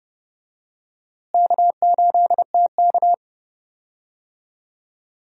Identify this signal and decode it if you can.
Morse code, heard as a keyed tone, about 30 wpm, 700 Hz. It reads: K8TK
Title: USA Call Signs - 4 Characters - Repeated 3x - 30wpm